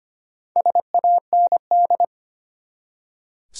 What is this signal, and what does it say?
Morse code, heard as a keyed tone, about 25 wpm, 700 Hz. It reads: SAND
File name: practice / 25wpm / Easy Words with Hints - 25wpm